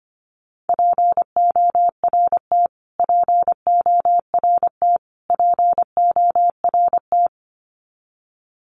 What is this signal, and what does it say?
Morse code, heard as a keyed tone, about 25 wpm, 700 Hz. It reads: PORT PORT PORT
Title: Top 500 Words Repeated 25wpm